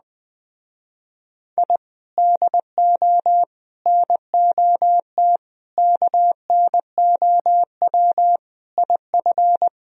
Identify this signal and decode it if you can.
Morse code, heard as a keyed tone, about 20 wpm, 700 Hz. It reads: I DO NOT KNOW IF